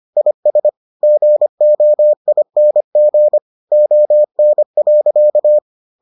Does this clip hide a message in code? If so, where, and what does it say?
Yes: Morse code from 0.2 to 5.6 s, heard as a keyed tone, about 25 wpm, 600 Hz. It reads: IS GOING ON.